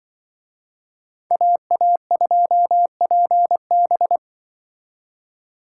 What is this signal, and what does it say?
Morse code, heard as a keyed tone, about 24 wpm, 700 Hz. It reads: AA2PB
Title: Speed Racing - USA 2x2 Call Signs 20wpm